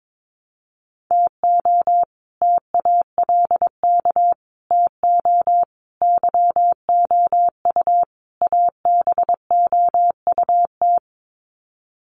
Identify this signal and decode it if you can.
Morse code, heard as a keyed tone, about 22 wpm, 700 Hz. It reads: TO TALK TO YOU ABOUT
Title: Sets of 5 Words Encore 22wpm